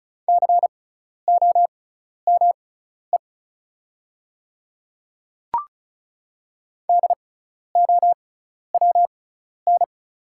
Morse code, heard as a keyed tone, about 36 wpm, spaced as for 12 wpm, 700 Hz. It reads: COME DOWN
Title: 6x Character Spacing - Top 100 Words - 35wpm